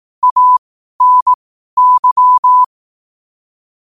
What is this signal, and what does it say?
Morse code, heard as a keyed tone, about 18 wpm, spaced as for 12 wpm, 1000 Hz. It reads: ANY
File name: qotdfiles / 12wpm